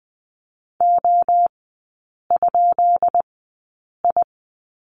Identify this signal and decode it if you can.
Morse code, heard as a keyed tone, about 20 wpm, spaced as for 8 wpm, 700 Hz. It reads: O?I
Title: 152 Lesson 25e - Letter p - Three Characters 20wpm